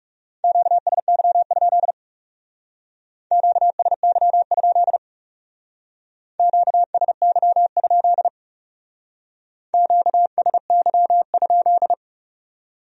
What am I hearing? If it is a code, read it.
Morse code, heard as a keyed tone, about 38 wpm, 700 Hz. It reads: QSY? QSY? QSY? QSY?